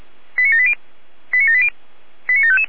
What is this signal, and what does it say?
Five-tone selective call (ZVEI), heard as quick runs of five tones, tones of about 70 ms. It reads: 89780 89780 89755